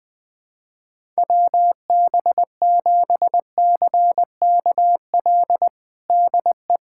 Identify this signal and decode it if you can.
Morse code, heard as a keyed tone, about 21 wpm, 700 Hz. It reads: WB7CKL DE
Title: USA Callsign DE Callsign - 6 Characters - 20wpm